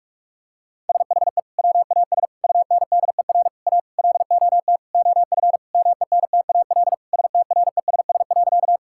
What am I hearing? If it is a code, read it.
Morse code, heard as a keyed tone, about 46 wpm, 700 Hz. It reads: SHE WAS UNDER A LOT OF MENTAL STRESS.
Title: Sentences from Top 1000 Words - Review Part 6 - 45wpm